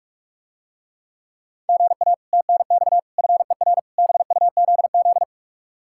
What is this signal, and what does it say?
Morse code, heard as a keyed tone, about 45 wpm, 700 Hz. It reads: GA TNX FER BUZZ